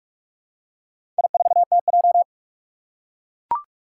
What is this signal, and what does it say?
Morse code, heard as a keyed tone, about 44 wpm, 700 Hz. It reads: I4TJ
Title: Four Challenging Letter-Number Combinations 45wpm